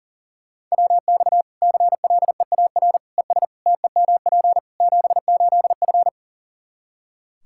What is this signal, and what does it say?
Morse code, heard as a keyed tone, about 40 wpm, 700 Hz. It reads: WX CLEAR ES TEMP 78F
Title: Practice QSO - Exchange 2 - Info - 40wpm